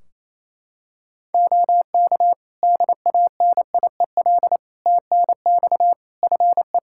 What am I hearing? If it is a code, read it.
Morse code, heard as a keyed tone, about 28 wpm, 700 Hz. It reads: OK DANIEL TNX FE
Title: K1USN SST QSO Practice 28wpm